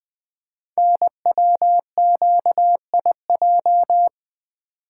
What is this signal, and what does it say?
Morse code, heard as a keyed tone, about 20 wpm, 700 Hz. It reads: NWQIJ
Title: Code Groups - 5 Letter:Numbers - 20wpm